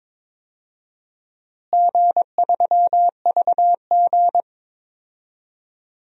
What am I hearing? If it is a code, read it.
Morse code, heard as a keyed tone, about 22 wpm, 700 Hz. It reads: G3VG
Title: DXCC-100 Call Signs - 4 Characters - Repeated 2x - 22wpm